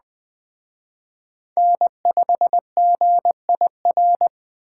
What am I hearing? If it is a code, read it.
Morse code, heard as a keyed tone, about 21 wpm, 700 Hz. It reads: N5GIR
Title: Speed Racing - USA 1x3 Call Signs 17wpm